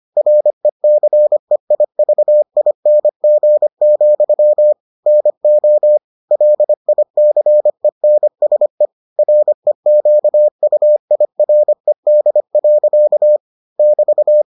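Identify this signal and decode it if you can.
Morse code, heard as a keyed tone, about 25 wpm, 600 Hz. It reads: RECEIVING, NO LICENSE REQUIRED. <BT>
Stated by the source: Morse edition of the Foundations of Amateur Radio episodes.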